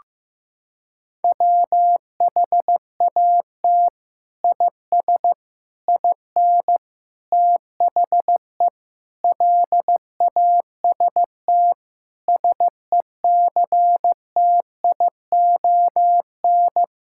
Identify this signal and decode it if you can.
Morse code, heard as a keyed tone, about 15 wpm, 700 Hz. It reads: WHAT IS IN THE LAST SECTION
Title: Sentences from Top 1000 Words - Review Part 5 - 15wpm